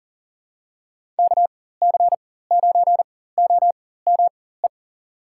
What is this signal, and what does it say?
Morse code, heard as a keyed tone, about 40 wpm, spaced as for 19 wpm, 700 Hz. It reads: KC9OME
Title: USA Call Signs - 6 Characters - 4x Character Spacing - 40wpm